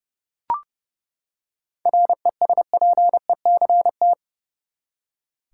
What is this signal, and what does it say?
Morse code, heard as a keyed tone, about 30 wpm, 700 Hz. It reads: RESPECT
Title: Seven Letter Words - 30wpm